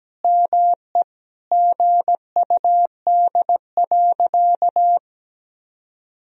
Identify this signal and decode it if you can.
Morse code, heard as a keyed tone, about 17 wpm, 700 Hz. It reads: ME GUD.
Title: Sentences from Top 500 Words and Abbreviations 17wpm